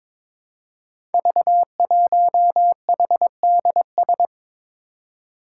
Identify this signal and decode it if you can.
Morse code, heard as a keyed tone, about 23 wpm, 700 Hz. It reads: V1HDS
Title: Five Challenging Letter-Number Combinations 22wpm